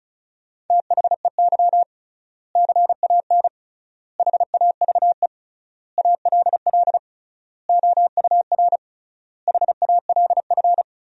Morse code, heard as a keyed tone, about 34 wpm, 700 Hz. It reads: THEY CAN HAVE ALL OUR HALF
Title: 3x Word-Spacing - Sentences from Top 400 Words - 35wpm